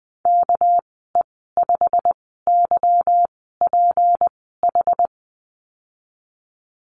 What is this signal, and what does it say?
Morse code, heard as a keyed tone, about 20 wpm, spaced as for 15 wpm, 700 Hz. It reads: KE5YPH
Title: 200 Lesson 32g - Letter k - Call Signs 20wpm